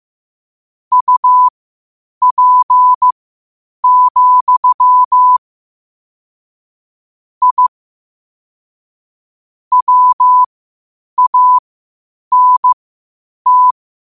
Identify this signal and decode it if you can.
Morse code, heard as a keyed tone, about 15 wpm, spaced as for 8 wpm, 1000 Hz. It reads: UP, I WANT
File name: qotdfiles / 8wpm